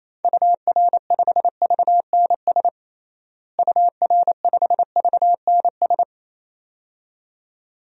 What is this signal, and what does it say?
Morse code, heard as a keyed tone, about 27 wpm, 700 Hz. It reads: UR5VNS UR5VNS
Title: DXCC-100 Call Signs - 6 Characters - Repeated 2x - 28wpm